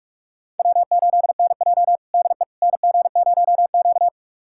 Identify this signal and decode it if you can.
Morse code, heard as a keyed tone, about 46 wpm, 700 Hz. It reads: W8NJ DE NG0Q
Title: USA Callsign DE Callsign - 4 Characters - 45wpm